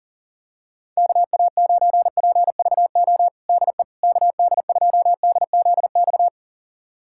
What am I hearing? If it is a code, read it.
Morse code, heard as a keyed tone, about 40 wpm, 700 Hz. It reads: KA9PVO DE KD2DZX
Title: USA Callsign DE Callsign - 6 Characters - 40wpm